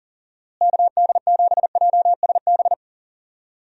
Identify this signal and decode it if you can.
Morse code, heard as a keyed tone, about 40 wpm, 700 Hz. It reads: KD7JSB
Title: USA Call Signs - 6 Characters - Repeated 3x - 40wpm